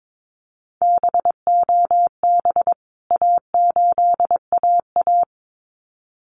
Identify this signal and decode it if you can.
Morse code, heard as a keyed tone, about 22 wpm, 700 Hz. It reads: BOB A8AA
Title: CWT Exchange 22wpm